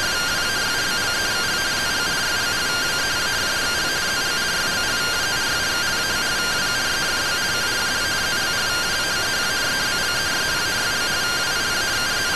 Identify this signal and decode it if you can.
Two-tone teleprinter signal, about 170 Hz shift.